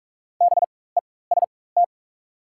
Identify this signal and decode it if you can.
Morse code, heard as a keyed tone, about 43 wpm, spaced as for 21 wpm, 690 Hz. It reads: BEST
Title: 4x Character Spacing - Top 500 Words - 45wpm